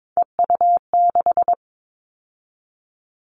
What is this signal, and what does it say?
Morse code, heard as a keyed tone, about 22 wpm, 700 Hz. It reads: EU6
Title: Callsign Prefixes for Top 50 DXCC - 22wpm